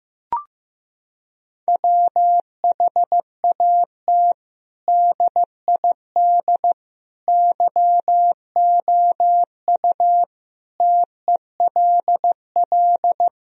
Morse code, heard as a keyed tone, about 15 wpm, 700 Hz. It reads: WHAT DID YOU TELL
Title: Sentences from Top 1000 Words - Review Part 4 - 15wpm